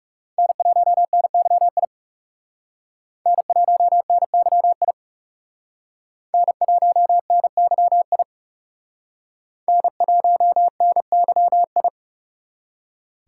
Morse code, heard as a keyed tone, about 39 wpm, 700 Hz. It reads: N1NYI N1NYI N1NYI N1NYI